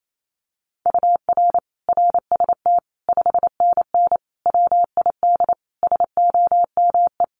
Morse code, heard as a keyed tone, about 28 wpm, 700 Hz. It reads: UR RST 5NN WID SOME
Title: Common QSO Phrases 28wpm